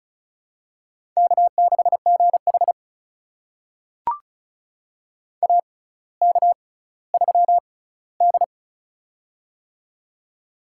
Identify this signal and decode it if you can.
Morse code, heard as a keyed tone, about 36 wpm, spaced as for 18 wpm, 700 Hz. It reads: K6GH AK3D
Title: USA Call Signs - 4 Characters - 6x Character Spacing - 35wpm